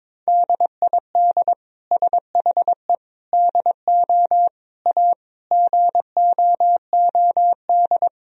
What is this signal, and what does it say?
Morse code, heard as a keyed tone, about 21 wpm, 700 Hz. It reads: DID SHE DO A GOOD
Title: Sentences from Top 1000 Words - Review Part 3 - 22wpm